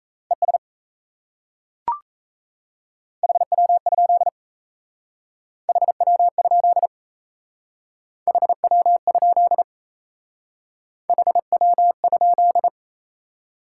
Morse code, heard as a keyed tone, about 36 wpm, 700 Hz. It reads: ES HW? HW? HW? HW?